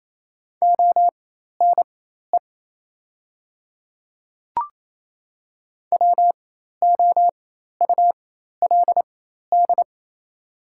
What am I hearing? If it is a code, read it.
Morse code, heard as a keyed tone, about 28 wpm, spaced as for 13 wpm, 700 Hz. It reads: ONE WOULD
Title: 4x Character Spacing - Top 100 Words - 28wpm